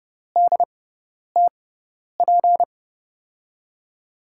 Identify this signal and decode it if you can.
Morse code, heard as a keyed tone, about 30 wpm, spaced as for 10 wpm, 700 Hz. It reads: DTP
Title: USA Call Signs - 5 Characters - 6x Character Spacing - 30wpm